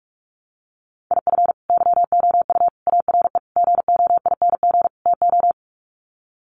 Morse code, heard as a keyed tone, about 44 wpm, 690 Hz. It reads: IF YOU ARE GOING TO
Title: Sets of 5 Words Encore2 45wpm